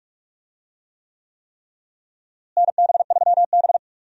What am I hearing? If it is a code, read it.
Morse code, heard as a keyed tone, about 46 wpm, 700 Hz. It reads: NB3B
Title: USA Call Signs - 4 Characters - Repeated 3x - 45wpm